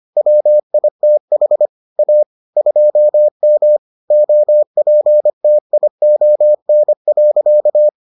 Morse code, heard as a keyed tone, about 25 wpm, 600 Hz. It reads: WITH A 2M OPTION.